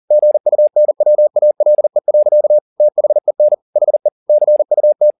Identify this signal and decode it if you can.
Morse code, heard as a keyed tone, about 40 wpm, 600 Hz. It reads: GUNWALE. THEN HE CUT